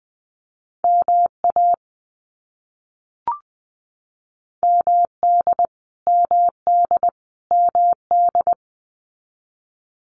Morse code, heard as a keyed tone, about 20 wpm, 700 Hz. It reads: MA MD MD MD
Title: US States Repeated 3x 20wpm